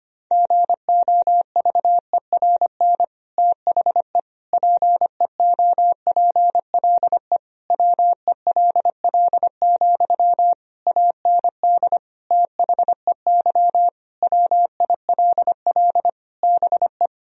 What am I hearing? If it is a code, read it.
Morse code, heard as a keyed tone, about 25 wpm, 700 Hz. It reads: GOVERN THE PEOPLE WELL, AND THEY WILL BE
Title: Sentences from Top 1000 Words - Review Part 2 - 25wpm